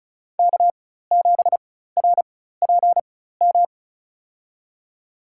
Morse code, heard as a keyed tone, about 36 wpm, spaced as for 16 wpm, 700 Hz. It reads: K7RPM
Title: USA Call Signs - 5 Characters - 4x Character Spacing - 35wpm